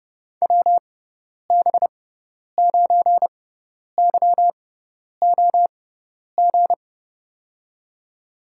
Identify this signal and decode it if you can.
Morse code, heard as a keyed tone, about 31 wpm, spaced as for 10 wpm, 700 Hz. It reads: WB9YOG
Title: USA Call Signs - 6 Characters - 6x Character Spacing - 30wpm